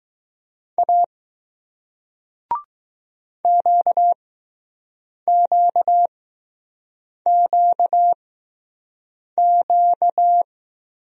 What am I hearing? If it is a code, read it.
Morse code, heard as a keyed tone, about 20 wpm, spaced as for 6 wpm, 700 Hz. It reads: A QQQQ